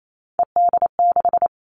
Morse code, heard as a keyed tone, about 28 wpm, 700 Hz. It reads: ED6
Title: Three Challenging Letter-Number Combinations 28wpm